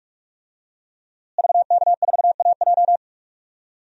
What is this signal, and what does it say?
Morse code, heard as a keyed tone, about 46 wpm, 700 Hz. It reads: VK4AJ
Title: DXCC-100 Call Signs - 5 Characters - Repeated 2x - 45wpm